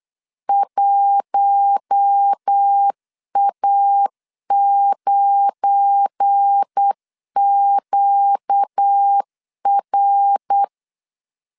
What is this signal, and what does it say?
Morse code, heard as a keyed tone, about 8 wpm, 800 Hz. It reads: 1A9QR
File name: CW / DET / 1 9 1500C S A 8wpm 5637